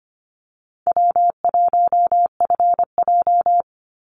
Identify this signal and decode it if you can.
Morse code, heard as a keyed tone, about 25 wpm, 700 Hz. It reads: W1FJ
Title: Easy Call Signs - 25wpm